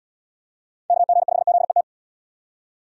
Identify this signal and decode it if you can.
Morse code, heard as a keyed tone, about 73 wpm, 700 Hz. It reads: KG5ZET